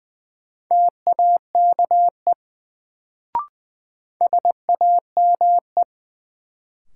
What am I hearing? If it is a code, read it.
Morse code, heard as a keyed tone, about 20 wpm, 700 Hz. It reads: TAKE SAME
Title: Sets of 2 Words - New Word Drill - 20wpm